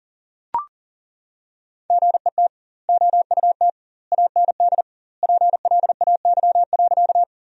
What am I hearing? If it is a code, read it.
Morse code, heard as a keyed tone, about 40 wpm, 700 Hz. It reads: GET OUT AND PLAY.
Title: 2x Word-Spacing - Sentences from Top 200 Words - 40wpm